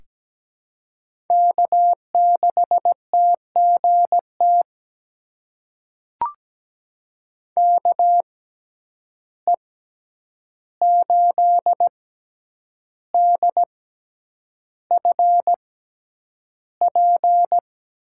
Morse code, heard as a keyed tone, about 17 wpm, spaced as for 8 wpm, 700 Hz. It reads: K6TGT KE8DFP
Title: USA Call Signs - 6x Character Spacing - 17wpm